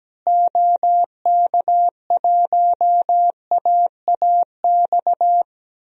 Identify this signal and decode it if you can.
Morse code, heard as a keyed tone, about 17 wpm, 700 Hz. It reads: OK1AAX
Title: DXCC-100 Call Signs - 6 Characters - Repeated 2x - 17wpm